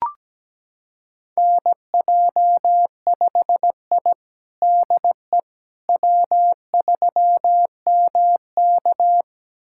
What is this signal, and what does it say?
Morse code, heard as a keyed tone, about 17 wpm, 700 Hz. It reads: NJ5I DE W3MK
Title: USA Callsign DE Callsign - 4 Characters - 17wpm